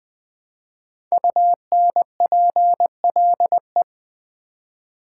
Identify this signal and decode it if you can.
Morse code, heard as a keyed tone, about 20 wpm, 700 Hz. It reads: UNPLE
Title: Build A Word - 5 Letters At A Time - 10-Letter Words - 20wpm